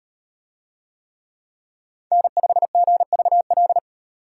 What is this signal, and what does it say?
Morse code, heard as a keyed tone, about 38 wpm, 700 Hz. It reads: N5GVL